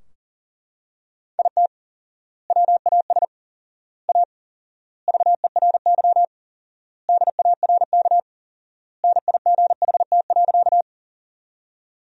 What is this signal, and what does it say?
Morse code, heard as a keyed tone, about 37 wpm, 700 Hz. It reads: IT WAS A VERY DARK NIGHT.
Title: 4x Word-Spacing - Sentences from Top 500 Words - 40wpm